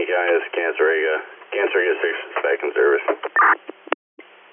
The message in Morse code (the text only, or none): none